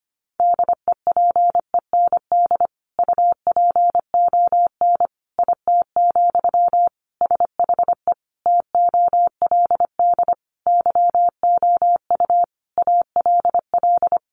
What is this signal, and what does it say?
Morse code, heard as a keyed tone, about 25 wpm, 700 Hz. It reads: DEPEND UPON IT, SHE TOLD YOU ALL
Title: Sentences from Top 1300 Words 25wpm